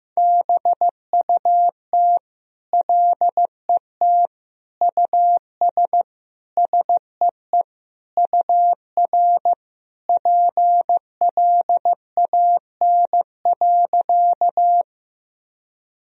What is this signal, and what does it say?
Morse code, heard as a keyed tone, about 15 wpm, 700 Hz. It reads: BUT LET US SEE UR PLAN.
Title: Sentences from Top 500 Words and Abbreviations 15wpm